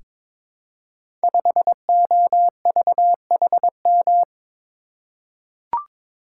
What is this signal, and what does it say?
Morse code, heard as a keyed tone, about 22 wpm, 690 Hz. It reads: 5OVHM
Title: Code Groups - 5 Letter:Numbers - 22wpm